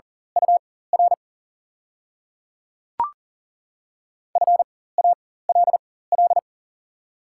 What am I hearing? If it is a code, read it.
Morse code, heard as a keyed tone, about 39 wpm, spaced as for 19 wpm, 700 Hz. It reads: UR FALL
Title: 4x Character Spacing - Top 500 Words - 40wpm